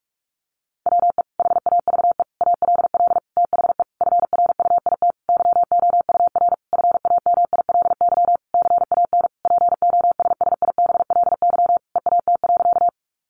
Morse code, heard as a keyed tone, about 44 wpm, 700 Hz. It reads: WE HAVE ALL THE FRUIT YOUR FAMILY CAN POSSIBLY EAT.
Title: Sentences from Top 1300 Words 45wpm